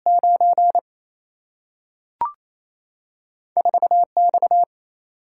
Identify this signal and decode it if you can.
Morse code, heard as a keyed tone, about 28 wpm, 700 Hz. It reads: 9 4 X
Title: Single-Letter Single-Number 28wpm